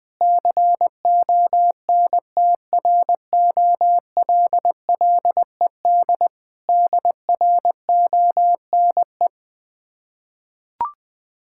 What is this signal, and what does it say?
Morse code, heard as a keyed tone, about 21 wpm, 690 Hz. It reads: CONTROLLED DRONE
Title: Difficult Words with Hints - 20wpm